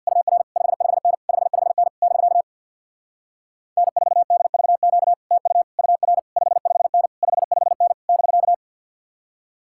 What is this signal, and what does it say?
Morse code, heard as a keyed tone, about 67 wpm, 700 Hz. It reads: UR 55N 55N <BK> N4DVQ TU UR 55N 55N <BK>